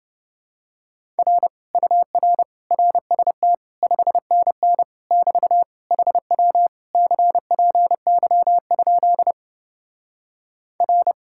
Morse code, heard as a keyed tone, about 28 wpm, 700 Hz. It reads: R UR RST 5NN <BT> HW CPY? R